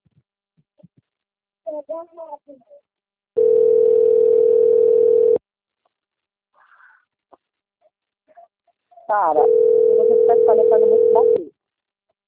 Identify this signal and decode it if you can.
Ringback tone (North America), 440 + 480 Hz, with speech.
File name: out- - - - - .60108